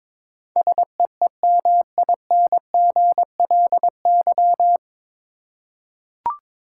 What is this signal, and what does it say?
Morse code, heard as a keyed tone, about 22 wpm, 700 Hz. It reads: SEEMINGLY
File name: 9 Letter Words - First 3 Letters Given - 22wpm